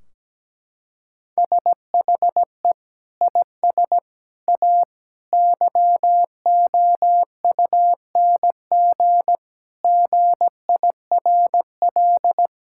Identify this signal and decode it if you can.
Morse code, heard as a keyed tone, about 17 wpm, 700 Hz. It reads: SHE IS A YOUNG GIRL
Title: Sentences from Top 1600 Words 17wpm